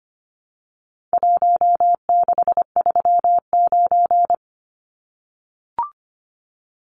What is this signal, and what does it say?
Morse code, heard as a keyed tone, about 25 wpm, 700 Hz. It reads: 1639
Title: Numbers - Contest Style - 4-Digit 25wpm